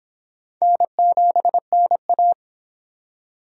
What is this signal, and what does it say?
Morse code, heard as a keyed tone, about 24 wpm, 700 Hz. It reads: N7NA